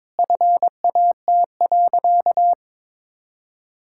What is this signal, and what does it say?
Morse code, heard as a keyed tone, about 22 wpm, 700 Hz. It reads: FAT.
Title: Sentences from Top 1000 Words - Review Part 5 - 22wpm